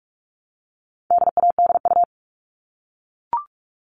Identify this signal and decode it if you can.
Morse code, heard as a keyed tone, about 44 wpm, 700 Hz. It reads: DADU
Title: All Cut Numbers - 4-Digit 45wpm